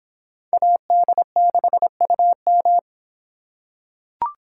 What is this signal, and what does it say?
Morse code, heard as a keyed tone, about 25 wpm, 700 Hz. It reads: AD6UM